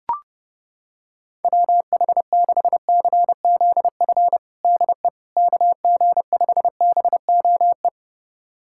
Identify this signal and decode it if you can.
Morse code, heard as a keyed tone, about 30 wpm, 700 Hz. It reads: WH6CZF DE KG5BOE
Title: USA Callsign DE Callsign - 6 Characters - 30wpm